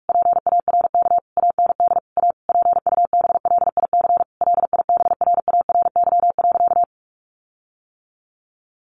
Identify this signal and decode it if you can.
Morse code, heard as a keyed tone, about 44 wpm, 700 Hz. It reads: PARK AND A PUBLIC LIBRARY.
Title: Sentences from Top 1300 Words 45wpm